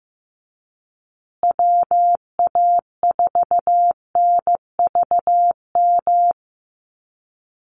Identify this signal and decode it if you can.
Morse code, heard as a keyed tone, about 15 wpm, 700 Hz. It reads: WA4NVM
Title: USA Call Signs Repeated 2x 15wpm